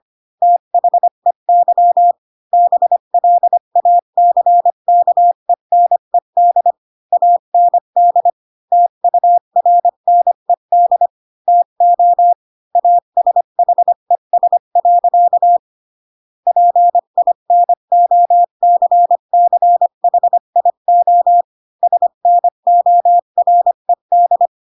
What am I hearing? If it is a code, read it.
Morse code, heard as a keyed tone, about 25 wpm, 700 Hz. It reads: THEY BLACKENED AND TURNED TO ASHES. PINOCCHIO SNORED